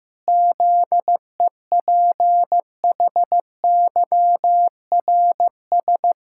Morse code, heard as a keyed tone, about 16 wpm, 700 Hz. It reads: ZEPHYRS
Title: Pangram Challenge v2 - 15wpm